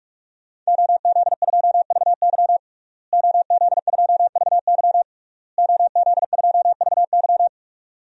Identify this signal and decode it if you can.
Morse code, heard as a keyed tone, about 46 wpm, 700 Hz. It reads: OZ2VY OZ2VY OZ2VY
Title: DXCC-100 Call Signs - 5 Characters - Repeated 3x - 45wpm